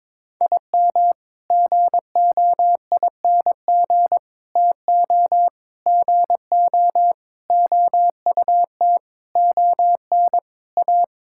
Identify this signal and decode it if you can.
Morse code, heard as a keyed tone, about 22 wpm, 700 Hz. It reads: IM GOING TO GO OUT ON A
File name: practice / 22wpm / Sets of 8 Words - 22wpm